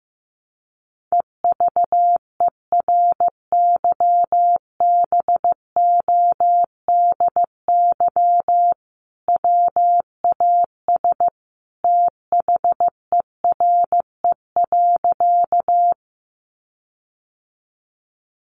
Morse code, heard as a keyed tone, about 15 wpm, 700 Hz. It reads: EVERYBODY WAS THERE.
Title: Sentences from Top 1400 Words 15wpm